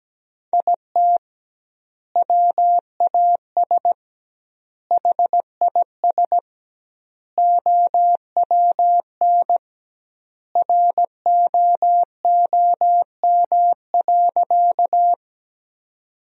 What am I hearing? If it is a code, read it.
Morse code, heard as a keyed tone, about 17 wpm, 700 Hz. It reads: IT WAS HIS OWN ROOM.
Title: 2x Word-Spacing - Sentences from Top 300 Words - 17wpm